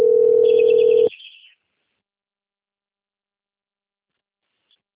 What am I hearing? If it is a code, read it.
Ringback tone (North America), 440 + 480 Hz.